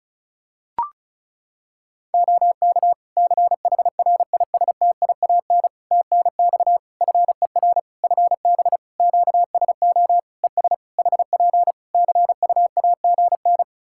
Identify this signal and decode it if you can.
Morse code, heard as a keyed tone, about 36 wpm, 700 Hz. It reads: OK CHRISTIAN TNX FER FB QSO ES HP CUAGN
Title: Practice QSO - Exchange 4 - Ending - 35wpm